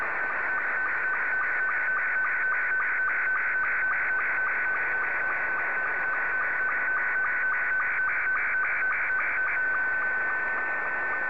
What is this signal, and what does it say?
SSTV (slow-scan television), Scottie 2.
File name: sstv / S2-VK7ZGK